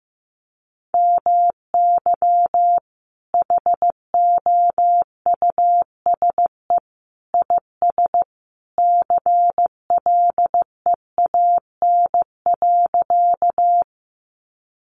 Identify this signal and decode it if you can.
Morse code, heard as a keyed tone, about 15 wpm, 700 Hz. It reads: MY HOUSE IS CLEAN.
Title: Sentences from Top 1100 Words 15wpm